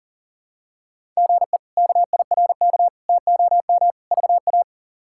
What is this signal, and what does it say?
Morse code, heard as a keyed tone, about 40 wpm, 700 Hz. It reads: GE KIRK TOM VA
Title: K1USN SST QSO Practice - 40wpm